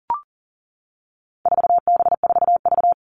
Morse code, heard as a keyed tone, about 38 wpm, 700 Hz. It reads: 4B4V
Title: All Cut Numbers - 4-Digit 40wpm